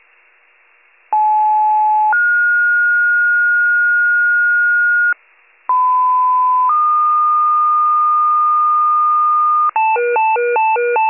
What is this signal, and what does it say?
Two-tone page, heard as a one-second tone, then a three-second tone, about 860 then 1430 Hz, about 1010 then 1220 Hz, with noise. It is heard from 1.1 to 9.7 s.
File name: pager / 83 RESCUE OFC / wcalert